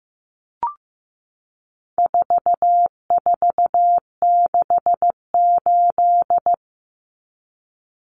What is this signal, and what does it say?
Morse code, heard as a keyed tone, about 15 wpm, 700 Hz. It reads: 4468
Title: Numbers - 4-Digit 15wpm